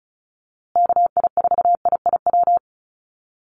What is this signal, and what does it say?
Morse code, heard as a keyed tone, about 34 wpm, 700 Hz. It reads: KI4IIW
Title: USA Call Signs Repeated 3x 35wpm